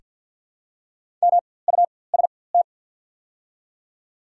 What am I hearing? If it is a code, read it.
Morse code, heard as a keyed tone, about 48 wpm, spaced as for 23 wpm, 700 Hz. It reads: MUST
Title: 4x Character Spacing - Top 300 Words - 50wpm